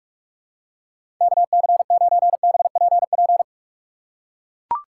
Morse code, heard as a keyed tone, about 43 wpm, 700 Hz. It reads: KC9BPP
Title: USA Call Signs - 6 Characters - 3x Character Spacing - 45wpm